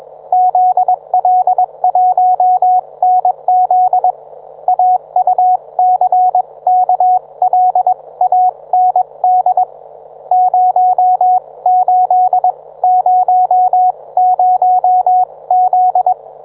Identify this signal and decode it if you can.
Morse code, heard as a keyed tone, about 21 wpm, spaced as for 18 wpm, 730 Hz, with radio noise. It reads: ZL1NZ AUCKLAND 0800Z
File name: book